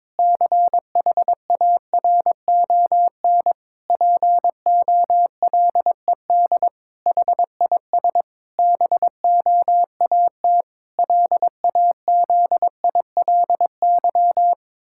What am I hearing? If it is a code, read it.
Morse code, heard as a keyed tone, about 22 wpm, 700 Hz. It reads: CHARON POLED HIS BOAT LAZILY
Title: Pangram Challenge v2 - 22wpm